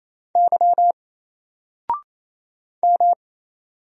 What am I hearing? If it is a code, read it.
Morse code, heard as a keyed tone, about 27 wpm, 700 Hz. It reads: Y M
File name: Single-Letter-Number-28wpm